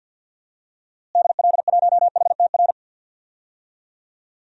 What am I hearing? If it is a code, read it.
Morse code, heard as a keyed tone, about 46 wpm, 700 Hz. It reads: DL1HTR
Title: DXCC-100 Call Signs - 6 Characters - Repeated 2x - 50wpm